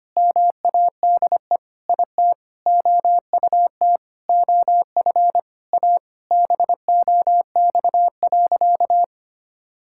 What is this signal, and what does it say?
Morse code, heard as a keyed tone, about 25 wpm, 700 Hz. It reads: MADE IT OUT OF A BOX.
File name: practice / 25wpm / Sentences with 3-Word Phrases - 25wpm